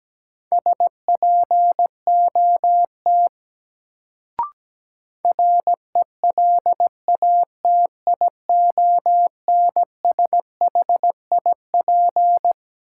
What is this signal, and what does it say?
Morse code, heard as a keyed tone, about 17 wpm, 700 Hz. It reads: SPOT RELATIONSHIP
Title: Sentences from Top 1100 Words - New Word Drill - 17wpm